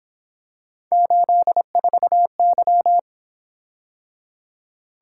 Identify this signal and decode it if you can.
Morse code, heard as a keyed tone, about 26 wpm, 700 Hz. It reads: 84Y
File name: Speed-Racing - Three Letter-Number Combinations - 22wpm